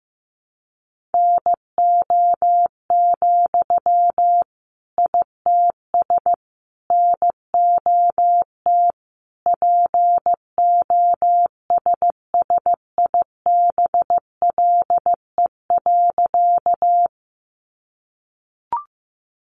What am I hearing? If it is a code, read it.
Morse code, heard as a keyed tone, about 15 wpm, 700 Hz. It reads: NO, ITS NOT POSSIBLE.
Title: Sentences from Top 600 Words 15wpm